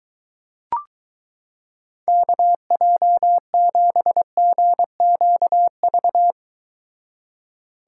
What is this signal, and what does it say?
Morse code, heard as a keyed tone, about 23 wpm, 700 Hz. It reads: KJ7GQV